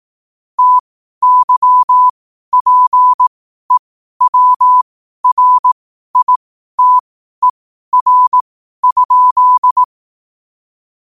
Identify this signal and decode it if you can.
Morse code, heard as a keyed tone, about 18 wpm, spaced as for 12 wpm, 1000 Hz. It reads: TYPEWRITER?
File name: qotdfiles / 12wpm